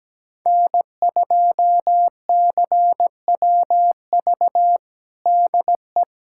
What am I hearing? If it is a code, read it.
Morse code, heard as a keyed tone, about 17 wpm, 700 Hz. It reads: N2CWV DE
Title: USA Callsign DE Callsign - 5 Characters - 17wpm